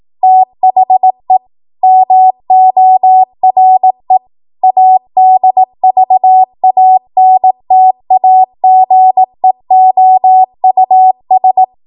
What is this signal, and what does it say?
Morse code, heard as a keyed tone, about 18 wpm, 750 Hz. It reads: THE MORE ADVANTAGEOUS
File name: Morse / 18WPM